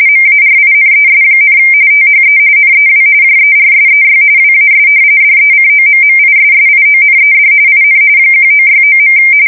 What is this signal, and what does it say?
RTTY, about 45.45 baud. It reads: MARINES, THE NEW JERSEY WAS DECOMMISSIONED IN 1969.
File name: fsk170 / battleshipnj